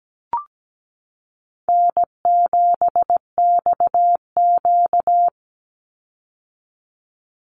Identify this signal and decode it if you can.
Morse code, heard as a keyed tone, about 17 wpm, 700 Hz. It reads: N7XQ
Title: USA Call Signs 17wpm